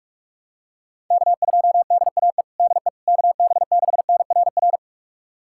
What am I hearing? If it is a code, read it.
Morse code, heard as a keyed tone, about 41 wpm, 690 Hz. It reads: K2DAE DE KB6NRR
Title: USA Callsign DE Callsign 45wpm